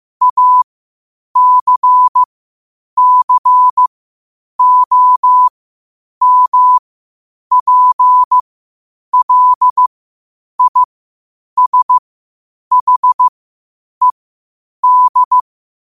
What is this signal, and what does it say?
Morse code, heard as a keyed tone, about 15 wpm, spaced as for 8 wpm, 1000 Hz. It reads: ACCOMPLISHED
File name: qotdfiles / 8wpm